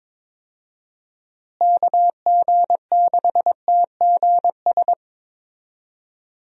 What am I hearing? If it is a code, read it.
Morse code, heard as a keyed tone, about 22 wpm, 700 Hz. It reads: KG6TGS
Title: USA Call Signs - 6 Characters - Repeated 2x - 22wpm